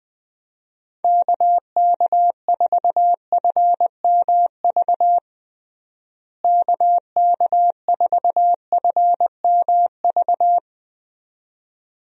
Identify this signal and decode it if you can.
Morse code, heard as a keyed tone, about 21 wpm, 700 Hz. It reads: KK4FMV KK4FMV
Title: USA Call Signs - 6 Characters - Repeated 2x - 20wpm